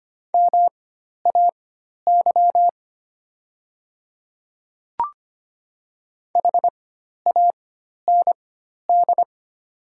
Morse code, heard as a keyed tone, about 25 wpm, spaced as for 12 wpm, 690 Hz. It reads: MAY HAND
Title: 4x Character Spacing - Top 200 Words - 25wpm